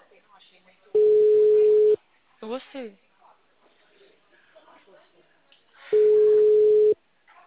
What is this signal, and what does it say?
Ringback tone (Europe, 425 Hz), with speech.
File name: out- -2007- - - .47283